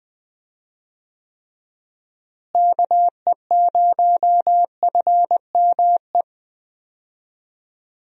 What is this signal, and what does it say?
Morse code, heard as a keyed tone, about 20 wpm, 700 Hz. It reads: KE0FME
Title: USA Call Signs - 6 Characters - Repeated 3x - 20wpm